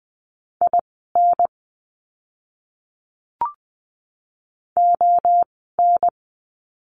Morse code, heard as a keyed tone, about 20 wpm, spaced as for 14 wpm, 700 Hz. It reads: IN ON
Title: 061 Lesson 11f - Letter h - Words 20wpm